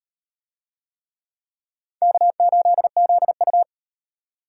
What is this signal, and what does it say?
Morse code, heard as a keyed tone, about 37 wpm, 700 Hz. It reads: K8ZU